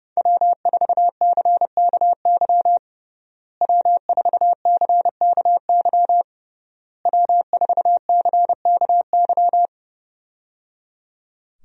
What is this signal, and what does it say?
Morse code, heard as a keyed tone, about 30 wpm, 700 Hz. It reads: W4CKY W4CKY W4CKY
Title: USA Call Signs - 5 Characters - Repeated 3x - 30wpm